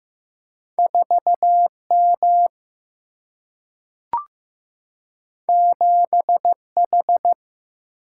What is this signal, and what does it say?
Morse code, heard as a keyed tone, about 15 wpm, 700 Hz. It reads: 4 M 7 H
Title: Single-Letter Single-Number 15wpm